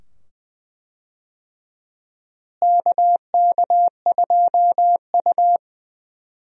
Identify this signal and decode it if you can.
Morse code, heard as a keyed tone, about 20 wpm, 700 Hz. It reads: KK2U
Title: USA Call Signs - 4 Characters - Repeated 2x - 20wpm